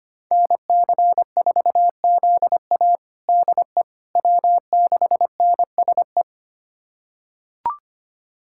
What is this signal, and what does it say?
Morse code, heard as a keyed tone, about 25 wpm, 700 Hz. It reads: NC4ZA DE W6NSE
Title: USA Callsign DE Callsign - 5 Characters - 25wpm